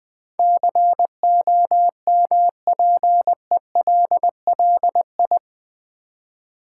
Morse code, heard as a keyed tone, about 20 wpm, 700 Hz. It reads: COMPELLI
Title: Build A Word - 4 Letters At A Time - 10-Letter Words - 20wpm